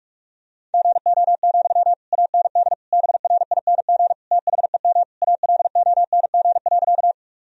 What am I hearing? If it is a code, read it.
Morse code, heard as a keyed tone, about 42 wpm, 700 Hz. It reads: GO, AND BRING THEM ALONG.
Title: Sentences from Top 1000 Words - Review Part 3 - 45wpm